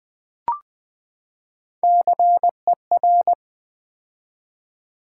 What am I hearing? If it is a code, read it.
Morse code, heard as a keyed tone, about 20 wpm, 700 Hz. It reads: CER
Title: Build A Word - 3 Letters At A Time - 9-Letter Words - 20wpm